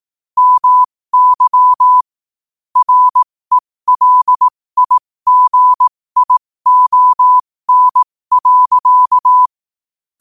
Morse code, heard as a keyed tone, about 18 wpm, spaced as for 15 wpm, 1000 Hz. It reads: MY RELIGION.
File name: qotdfiles / 15wpm